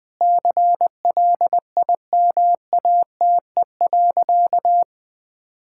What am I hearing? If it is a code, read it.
Morse code, heard as a keyed tone, about 21 wpm, 700 Hz. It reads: CLIMATE.
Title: Sentences from Top 1600 Words 20wpm